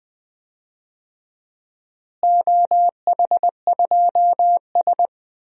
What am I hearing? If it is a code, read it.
Morse code, heard as a keyed tone, about 20 wpm, 700 Hz. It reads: OH2S
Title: DXCC-100 Call Signs - 4 Characters - Repeated 2x - 20wpm